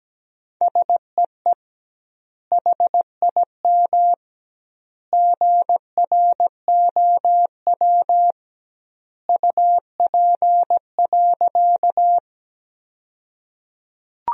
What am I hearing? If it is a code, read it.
Morse code, heard as a keyed tone, about 17 wpm, 700 Hz. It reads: SEE HIM GROW UP.
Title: 2x Word-Spacing - Sentences from Top 300 Words - 17wpm